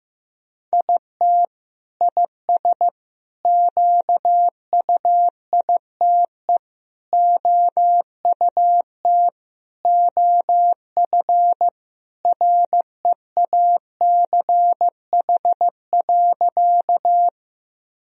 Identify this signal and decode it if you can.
Morse code, heard as a keyed tone, about 15 wpm, 700 Hz. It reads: IT IS QUITE OUT OF REACH.
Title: Sentences from Top 1000 Words - Review Part 2 - 15wpm